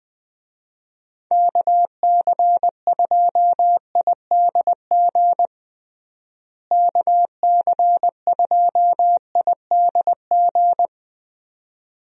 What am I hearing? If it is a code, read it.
Morse code, heard as a keyed tone, about 20 wpm, 700 Hz. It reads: KC2IDG KC2IDG
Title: USA Call Signs - 6 Characters - Repeated 2x - 20wpm